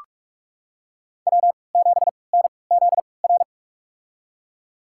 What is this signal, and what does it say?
Morse code, heard as a keyed tone, about 41 wpm, spaced as for 25 wpm, 700 Hz. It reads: W7NZR
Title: USA Call Signs - 5 Characters - 3x Character Spacing - 45wpm